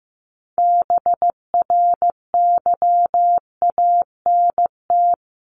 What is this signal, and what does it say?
Morse code, heard as a keyed tone, about 15 wpm, 700 Hz. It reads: BRYANT
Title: CWT Exchange 15wpm